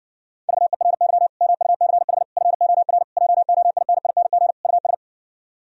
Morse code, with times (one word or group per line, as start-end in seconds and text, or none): 0.5-1.3 VERY
1.4-2.2 MUCH
2.4-3.0 FOR
3.2-4.5 JOINING
4.6-4.9 US